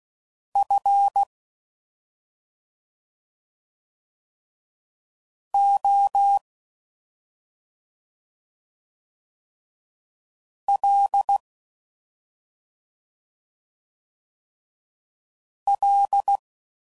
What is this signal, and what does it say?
Morse code, heard as a keyed tone, about 16 wpm, 800 Hz. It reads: F O L L